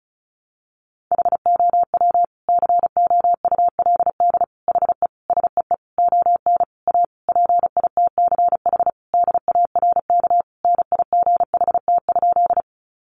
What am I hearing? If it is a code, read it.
Morse code, heard as a keyed tone, about 34 wpm, 700 Hz. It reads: HOW COULD HE SEE ON A PITCH DARK NIGHT?
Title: Sentences from Top 1300 Words 35wpm